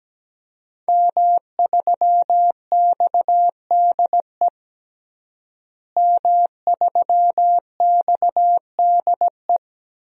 Morse code, heard as a keyed tone, about 17 wpm, 700 Hz. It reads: M3XDE M3XDE
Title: DXCC-100 Call Signs - 5 Characters - Repeated 2x - 17wpm